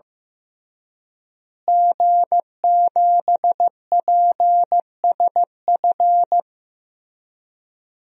Morse code, heard as a keyed tone, about 16 wpm, 700 Hz. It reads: G7PSF
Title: DXCC-100 Call Signs - 5 Characters - Repeated 3x - 15wpm